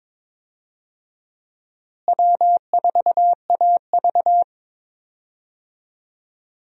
Morse code, heard as a keyed tone, about 22 wpm, 700 Hz. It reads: W4AV
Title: USA Call Signs - 4 Characters - Repeated 3x - 22wpm